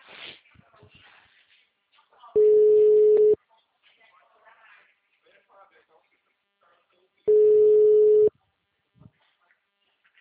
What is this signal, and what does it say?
Ringback tone (Europe, 425 Hz), with speech.